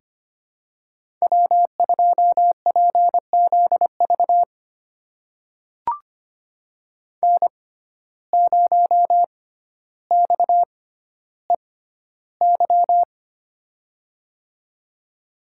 Morse code, heard as a keyed tone, about 25 wpm, spaced as for 13 wpm, 700 Hz. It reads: W2PZV N0XEY
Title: USA Call Signs - 5 Characters - 6x Character Spacing - 25wpm